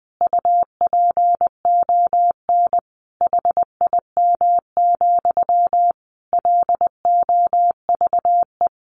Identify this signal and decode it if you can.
Morse code, heard as a keyed tone, about 20 wpm, 700 Hz. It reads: UPON HIM, LOVE
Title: Sentences from Top 1100 Words 20wpm